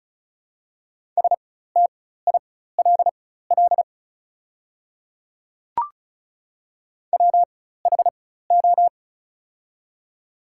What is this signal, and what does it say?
Morse code, heard as a keyed tone, about 36 wpm, spaced as for 16 wpm, 690 Hz. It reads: STILL WHO
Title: 4x Character Spacing - Top 300 Words - 35wpm